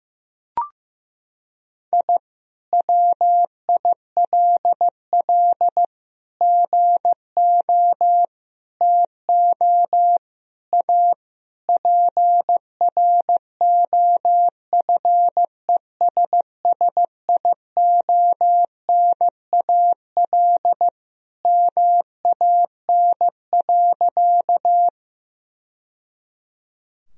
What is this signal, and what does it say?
Morse code, heard as a keyed tone, about 15 wpm, 700 Hz. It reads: I WILL GO TO A PROFESSIONAL MAN.
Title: Sentences from Top 1000 Words - Review Part 4 - 15wpm